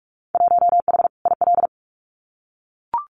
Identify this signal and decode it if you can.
Morse code, heard as a keyed tone, about 44 wpm, 700 Hz. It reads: 1H IL
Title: Winter Field Day Exchanges 45wpm